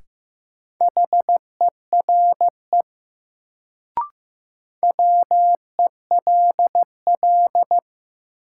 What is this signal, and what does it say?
Morse code, heard as a keyed tone, about 15 wpm, 700 Hz. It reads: HERE WELL
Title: Sentences from Top 200 Words - New Word Drill - 15wpm